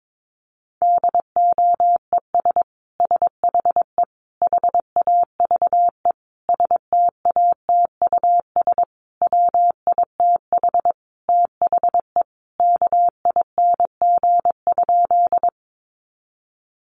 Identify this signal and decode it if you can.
Morse code, heard as a keyed tone, about 22 wpm, 700 Hz. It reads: DOES SHE HAVE STATUS WITH THE KING?
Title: Sentences from Top 1400 Words 22wpm